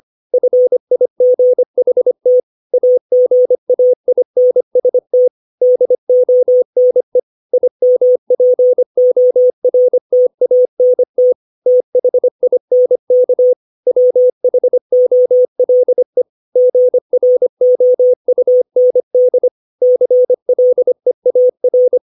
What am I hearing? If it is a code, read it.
Morse code, heard as a keyed tone, about 25 wpm, 500 Hz. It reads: FIGHT AGAINST DONE IMPORTANT THINK WHOLE GROUND CLEAR
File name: 632 words / 632 Words 25WPM